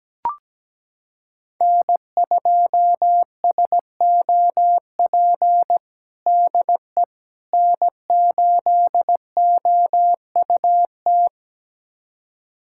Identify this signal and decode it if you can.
Morse code, heard as a keyed tone, about 17 wpm, 700 Hz. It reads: N2SOP DE N8OUT
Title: USA Callsign DE Callsign - 5 Characters - 17wpm